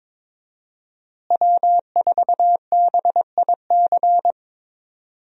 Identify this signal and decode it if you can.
Morse code, heard as a keyed tone, about 22 wpm, 700 Hz. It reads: W4BIC
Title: USA Call Signs - 5 Characters - Repeated 3x - 22wpm